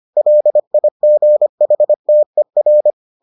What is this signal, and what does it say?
Morse code, heard as a keyed tone, about 25 wpm, 600 Hz. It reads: LIGHTER